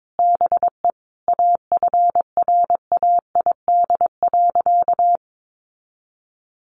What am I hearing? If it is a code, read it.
Morse code, heard as a keyed tone, about 22 wpm, 700 Hz. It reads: BE AFRAID.
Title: Sentences from Top 1500 Words 22wpm